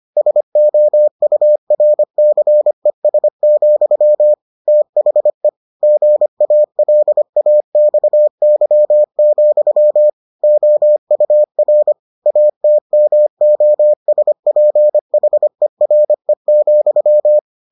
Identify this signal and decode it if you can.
Morse code, heard as a keyed tone, about 25 wpm, 600 Hz. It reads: SOURCES, THE GALAXY, OUR ATMOSPHERE,